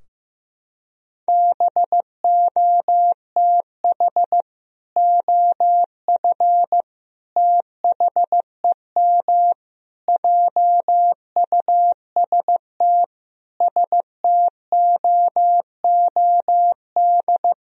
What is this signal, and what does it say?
Morse code, heard as a keyed tone, about 15 wpm, 700 Hz. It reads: BOTH OF THEM JUST STOOD
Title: Sentences from Top 1000 Words - Review Part 2 - 15wpm